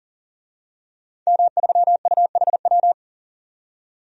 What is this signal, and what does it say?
Morse code, heard as a keyed tone, about 38 wpm, 700 Hz. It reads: M3UHW
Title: DXCC-100 Call Signs - 5 Characters - 40wpm